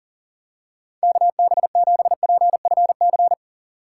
Morse code, heard as a keyed tone, about 40 wpm, 700 Hz. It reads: KB7PFC
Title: USA Call Signs - 3x Character Spacing - 40wpm